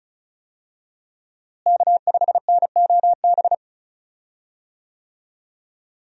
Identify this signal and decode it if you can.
Morse code, heard as a keyed tone, about 36 wpm, 700 Hz. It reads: K5NOB
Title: USA Call Signs - 5 Characters - Repeated 3x - 35wpm